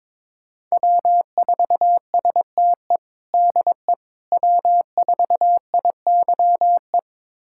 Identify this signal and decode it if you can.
Morse code, heard as a keyed tone, about 22 wpm, 700 Hz. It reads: W4STE DE W4IYE
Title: USA Callsign DE Callsign - 5 Characters - 22wpm